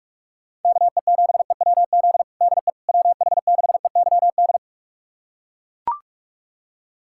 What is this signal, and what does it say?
Morse code, heard as a keyed tone, about 41 wpm, 700 Hz. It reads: KE7EWZ DE WH6EYD
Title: USA Callsign DE Callsign 45wpm